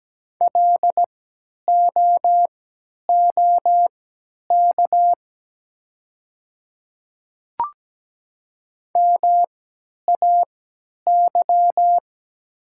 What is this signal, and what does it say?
Morse code, heard as a keyed tone, about 17 wpm, spaced as for 10 wpm, 700 Hz. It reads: LOOK MAY
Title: 3x Character Spacing - Top 500 Words - 17wpm